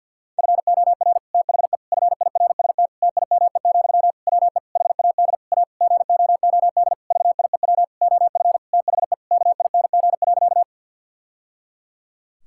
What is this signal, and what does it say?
Morse code, heard as a keyed tone, about 50 wpm, 700 Hz. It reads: FOR THE FIRST TIME, WE HAD A GOOD VIEW OF THE KING.
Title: Sentences from Top 1000 Words - Review Part 6 - 50wpm